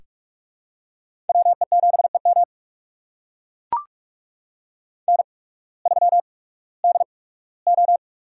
Morse code, heard as a keyed tone, about 44 wpm, spaced as for 21 wpm, 700 Hz. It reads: WE7EM N3DO
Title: USA Call Signs - 5 Characters - 8x Character Spacing - 45wpm